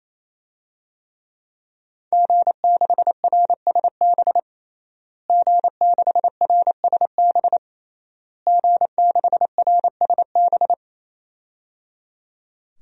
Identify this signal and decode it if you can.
Morse code, heard as a keyed tone, about 28 wpm, 700 Hz. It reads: G6RSB G6RSB G6RSB
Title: DXCC-100 Call Signs - 5 Characters - Repeated 3x - 28wpm